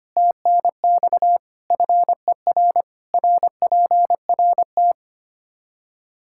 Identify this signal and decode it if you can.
Morse code, heard as a keyed tone, about 25 wpm, 700 Hz. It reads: TNX FER RPRT
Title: Practice QSO - Exchange 2 - Info - 25wpm